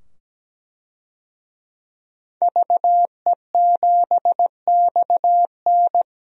Morse code, heard as a keyed tone, about 17 wpm, 700 Hz. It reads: VE7XN
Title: DXCC-100 Call Signs - 5 Characters - 17wpm